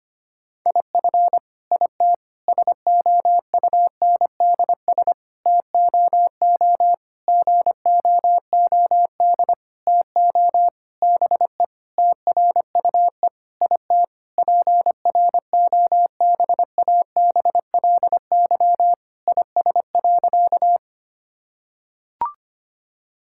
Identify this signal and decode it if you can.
Morse code, heard as a keyed tone, about 25 wpm, 700 Hz. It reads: IF IT SOUNDS TOO GOOD TO BE TRUE IT PROBABLY IS.
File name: practice / 25wpm / Sentences with 9-Word Phrases - 25wpm